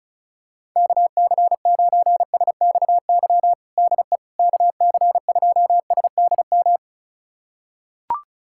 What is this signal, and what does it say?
Morse code, heard as a keyed tone, about 36 wpm, 700 Hz. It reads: KC9SXY DE KC2SDM
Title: USA Callsign DE Callsign - 6 Characters - 35wpm